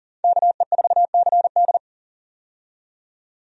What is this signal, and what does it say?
Morse code, heard as a keyed tone, about 40 wpm, 700 Hz. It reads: KE4CD
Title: USA Call Signs - 5 Characters - 40wpm